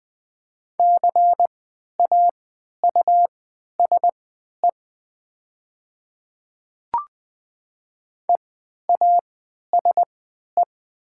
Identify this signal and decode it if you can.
Morse code, heard as a keyed tone, about 20 wpm, spaced as for 11 wpm, 700 Hz. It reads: CAUSE EASE
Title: 3x Character Spacing - Top 300 Words - 20wpm